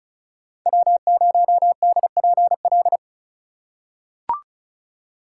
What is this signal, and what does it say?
Morse code, heard as a keyed tone, about 36 wpm, 700 Hz. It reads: W0DPL
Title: USA Call Signs - 5 Characters - 2x Character Spacing - 35wpm